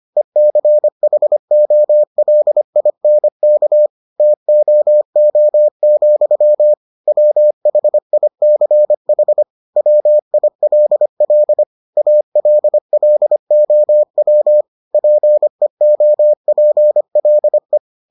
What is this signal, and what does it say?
Morse code, heard as a keyed tone, about 25 wpm, 600 Hz. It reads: ECHOLINK TOO, WHICH WILL ALLOW PEOPLE